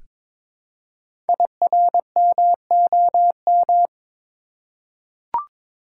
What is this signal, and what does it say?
Morse code, heard as a keyed tone, about 22 wpm, 700 Hz. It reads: IRMOM
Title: Code Groups - 5 Letters - 22wpm